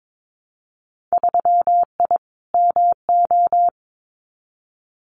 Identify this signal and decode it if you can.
Morse code, heard as a keyed tone, about 22 wpm, 700 Hz. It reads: 3I MO
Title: Winter Field Day Exchanges 22wpm